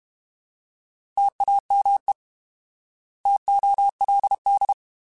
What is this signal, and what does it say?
Morse code, heard as a keyed tone, about 32 wpm, 800 Hz. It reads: TAME TOLD
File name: Alphabetical / Alphabetical mp3s / Ts @30